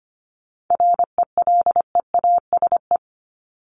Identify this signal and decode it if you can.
Morse code, heard as a keyed tone, about 25 wpm, 700 Hz. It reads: RELEASE
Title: Top 1001-1250 Words 25wpm